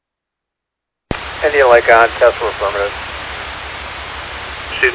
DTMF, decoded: none

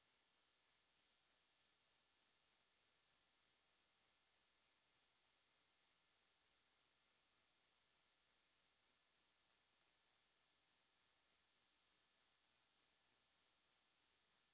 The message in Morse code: none

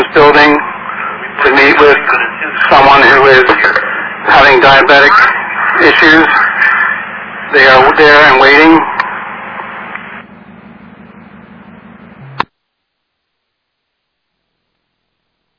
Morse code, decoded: none

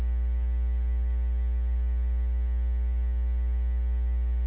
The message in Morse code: none

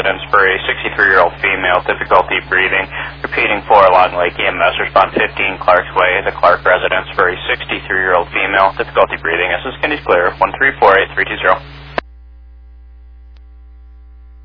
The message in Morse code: none

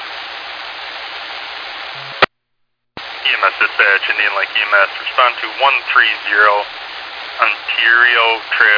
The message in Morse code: none

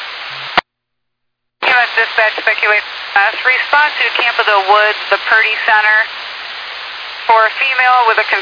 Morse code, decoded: none

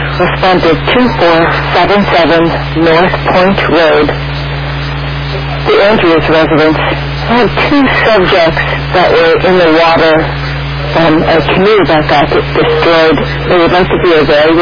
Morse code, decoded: none

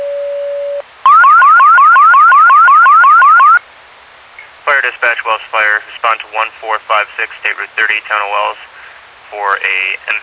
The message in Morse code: none

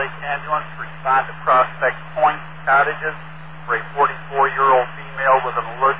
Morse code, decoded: none

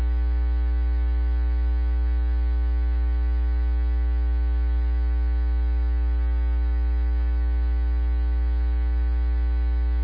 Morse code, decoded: none

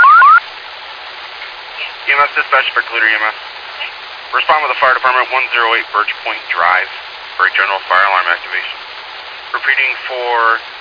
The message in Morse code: none

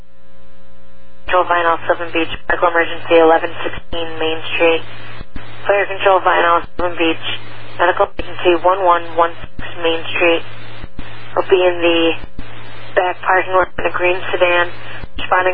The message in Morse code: none

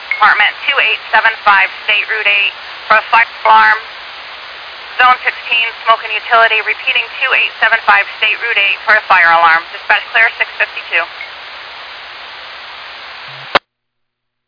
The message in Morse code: none

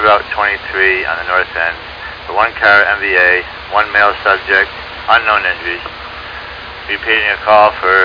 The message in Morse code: none

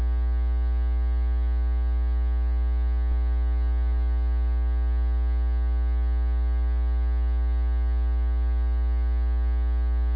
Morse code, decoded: none

